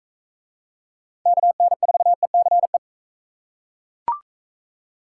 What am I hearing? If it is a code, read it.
Morse code, heard as a keyed tone, about 39 wpm, 700 Hz. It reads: KN4ECE